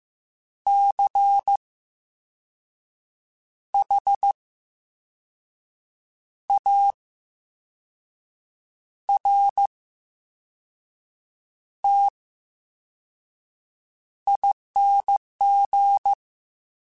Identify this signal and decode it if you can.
Morse code, heard as a keyed tone, about 15 wpm, spaced as for 5 wpm, 800 Hz. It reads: CHARTING
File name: Special INGs-mp3s / Special INGs-mp3s / VerySlowToFaster Ing @13